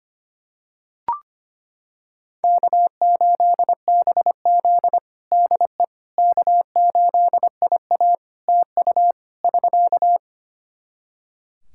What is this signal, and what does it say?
Morse code, heard as a keyed tone, about 25 wpm, 700 Hz. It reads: K8BZ DE K8IA TU <SK>
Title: K1USN SST QSO Practice 25wpm